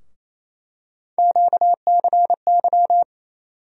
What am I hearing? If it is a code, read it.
Morse code, heard as a keyed tone, about 27 wpm, 700 Hz. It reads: QCY
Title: USA Call Sign Postfixes - 3 Letters 28wpm